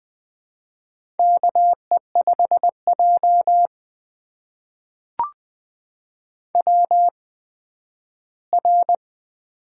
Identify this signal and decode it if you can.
Morse code, heard as a keyed tone, about 20 wpm, spaced as for 12 wpm, 700 Hz. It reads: KE5J WR
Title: USA Call Signs - 4 Characters - 8x Character Spacing - 20wpm